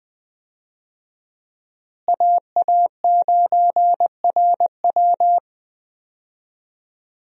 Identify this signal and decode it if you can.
Morse code, heard as a keyed tone, about 20 wpm, 700 Hz. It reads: AA9RW
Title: USA Call Signs - 5 Characters - Repeated 3x - 20wpm